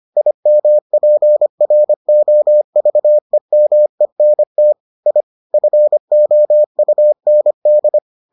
Morse code, heard as a keyed tone, about 25 wpm, 600 Hz. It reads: IMPROVEMENT I FOUND